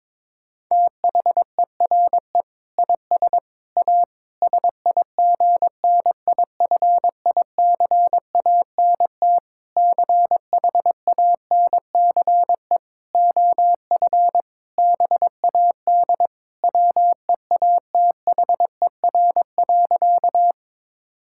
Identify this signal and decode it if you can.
Morse code, heard as a keyed tone, about 22 wpm, 700 Hz. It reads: THERE IS A SIGNIFICANT CHANCE OF BAD WEATHER.
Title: Sentences from Top 1000 Words - Review Part 6 - 22wpm